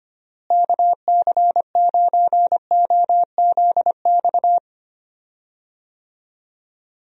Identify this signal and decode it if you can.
Morse code, heard as a keyed tone, about 25 wpm, 700 Hz. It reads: KC9OZX
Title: USA Call Signs - 6 Characters - Repeated 2x - 25wpm